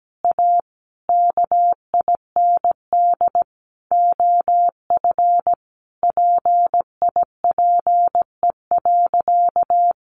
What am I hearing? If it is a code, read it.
Morse code, heard as a keyed tone, about 17 wpm, 700 Hz. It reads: A KIND OF PIPE.
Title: Sentences from Top 1500 Words 17wpm